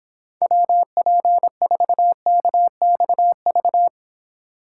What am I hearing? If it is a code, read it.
Morse code, heard as a keyed tone, about 25 wpm, 700 Hz. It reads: WP4KXV